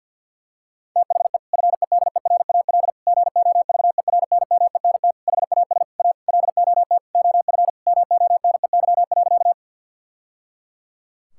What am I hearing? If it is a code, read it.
Morse code, heard as a keyed tone, about 49 wpm, 700 Hz. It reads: THE FEDERAL GOVERNMENT HAS A LOT OF MONEY.
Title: Sentences from Top 1000 Words - Review Part 5 - 50wpm